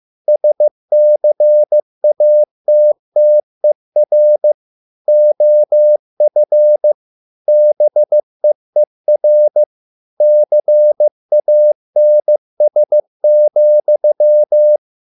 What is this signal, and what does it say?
Morse code, heard as a keyed tone, about 15 wpm, 600 Hz. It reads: SCATTER OF BEER CANS,